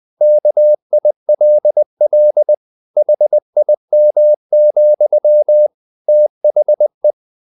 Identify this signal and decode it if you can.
Morse code, heard as a keyed tone, about 20 wpm, 600 Hz. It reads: KILL HIM, THE
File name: the old man and the sea part 2 20wpm